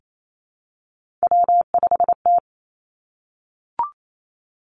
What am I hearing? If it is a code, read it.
Morse code, heard as a keyed tone, about 28 wpm, 700 Hz. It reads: W5T
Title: Three Challenging Letter-Number Combinations 28wpm